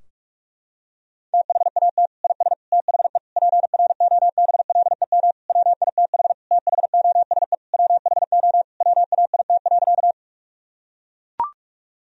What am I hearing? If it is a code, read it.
Morse code, heard as a keyed tone, about 41 wpm, 700 Hz. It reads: THAT IS THE PROBLEM WITH THOSE WHO WAIT.
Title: Sentences from Top 1000 Words - Review Part 2 - 45wpm